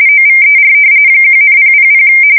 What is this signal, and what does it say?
RTTY, about 45.45 baud. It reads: OVER 1,000
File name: fsk170 / usnavy5